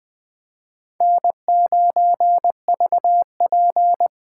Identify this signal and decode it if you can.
Morse code, heard as a keyed tone, about 20 wpm, 700 Hz. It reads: N9VP
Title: USA Call Signs - 6x Character Spacing - 20wpm